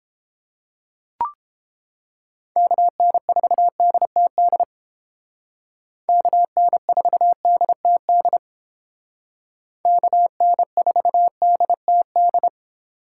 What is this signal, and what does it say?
Morse code, heard as a keyed tone, about 29 wpm, 700 Hz. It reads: KN4DTD KN4DTD KN4DTD